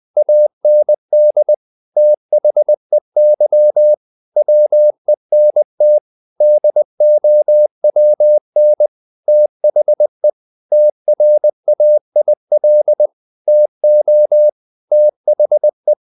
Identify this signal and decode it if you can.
Morse code, heard as a keyed tone, about 20 wpm, 600 Hz. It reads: AND THEY WENT DOWN THE TRAIL TO THE
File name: the old man and the sea / the old man and the sea part 1 20wpm